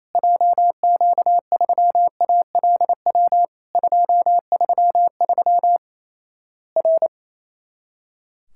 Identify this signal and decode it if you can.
Morse code, heard as a keyed tone, about 28 wpm, 700 Hz. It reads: JQ3ALW 233 R
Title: Worked All Europe DX Contest - QTC Exchange - 28wpm